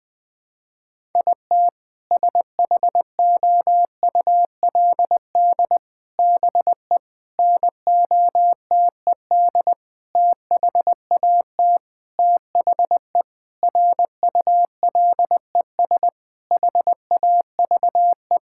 Morse code, heard as a keyed tone, about 21 wpm, 700 Hz. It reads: IT SHOULD BE NOTED THAT THE RULES HAVE
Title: Sentences with 5-Word Phrases - 20wpm